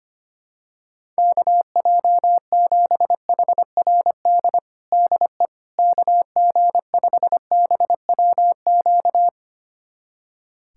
Morse code, heard as a keyed tone, about 25 wpm, 700 Hz. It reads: KJ7HRD DE KG5BWQ
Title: USA Callsign DE Callsign 25wpm